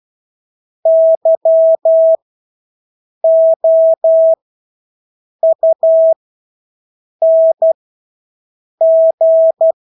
Morse code, heard as a keyed tone, about 12 wpm, spaced as for 6 wpm, 650 Hz. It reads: YOUNG
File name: Engelske ord 12-6 wpm